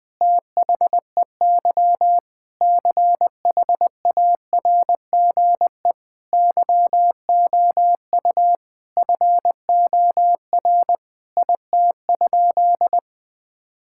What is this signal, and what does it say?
Morse code, heard as a keyed tone, about 21 wpm, 700 Hz. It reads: THEY CHARGE YOU FOR IT?
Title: Sentences from Top 1000 Words - Review Part 6 - 20wpm